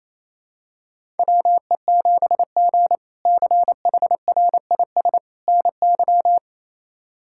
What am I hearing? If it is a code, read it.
Morse code, heard as a keyed tone, about 28 wpm, 700 Hz. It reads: WE7G CHRIS NY
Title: K1USN SST QSO Practice - 28wpm